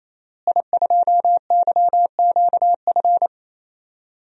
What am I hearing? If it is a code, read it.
Morse code, heard as a keyed tone, about 28 wpm, 700 Hz. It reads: I2YQF
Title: DXCC-100 Call Signs - 5 Characters - Repeated 3x - 28wpm